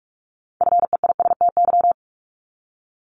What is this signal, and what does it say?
Morse code, heard as a keyed tone, about 46 wpm, 700 Hz. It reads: FEISTY
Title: Multi-letter Phonemes Repeated 3x 45wpm